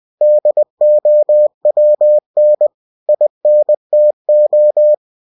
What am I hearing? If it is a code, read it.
Morse code, heard as a keyed tone, about 20 wpm, 600 Hz. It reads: DOWN INTO
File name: cw-books / winnie-the-pooh / winnie-20wpm-page-0095